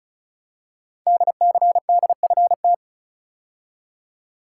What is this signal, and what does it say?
Morse code, heard as a keyed tone, about 36 wpm, 700 Hz. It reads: DCDFT
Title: Code Groups - 5 Letters - 35wpm